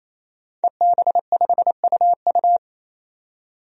Morse code, heard as a keyed tone, about 27 wpm, 700 Hz. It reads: EB5UU
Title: DXCC-100 Call Signs - 5 Characters - 28wpm